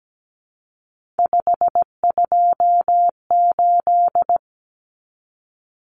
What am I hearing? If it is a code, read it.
Morse code, heard as a keyed tone, about 17 wpm, 690 Hz. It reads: 528
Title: Common Cut Numbers - 3-Digit 17wpm